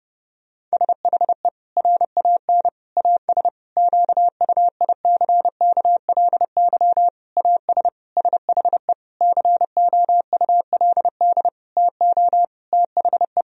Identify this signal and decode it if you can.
Morse code, heard as a keyed tone, about 30 wpm, 700 Hz. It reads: SHE RAN AS QUICKLY AS SHE COULD TO THE
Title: Sentences from Top 1000 Words - Review Part 4 - 30wpm